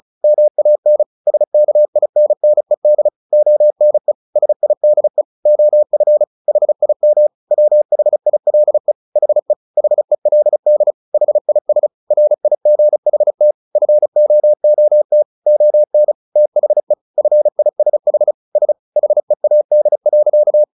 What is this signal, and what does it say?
Morse code, heard as a keyed tone, about 34 wpm, 600 Hz. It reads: MAN SKINNED ONE SIDE OF HIM WHILE HE HELD HIS RIGHT FOOT ON THE FISH S HEAD.
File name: the old man and the sea 2 35wpm